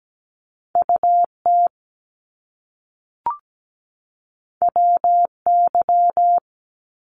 Morse code, heard as a keyed tone, about 17 wpm, 700 Hz. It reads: UT WY
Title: ARRL Sections 17wpm